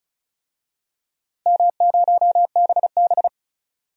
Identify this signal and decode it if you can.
Morse code, heard as a keyed tone, about 34 wpm, 700 Hz. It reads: M0BB
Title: DXCC-100 Call Signs - 4 Characters - Repeated 3x - 35wpm